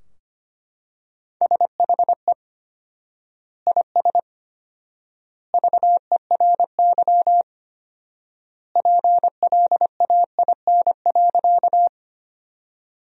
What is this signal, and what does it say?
Morse code, heard as a keyed tone, about 25 wpm, 700 Hz. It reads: SHE IS VERY PLAIN.
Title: 4x Word-Spacing - Sentences from Top 400 Words - 25wpm